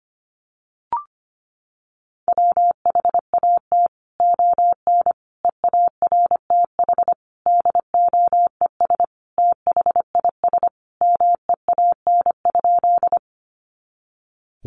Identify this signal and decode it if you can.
Morse code, heard as a keyed tone, about 25 wpm, 700 Hz. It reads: WHAT ON EARTH DOES THIS MEAN?
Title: Sentences from Top 200 Words 25wpm